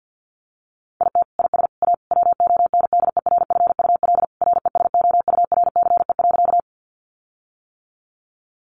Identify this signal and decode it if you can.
Morse code, heard as a keyed tone, about 50 wpm, 700 Hz. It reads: IT IS A WONDERFUL RESOURCE.
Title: Sentences from Top 1300 Words 50wpm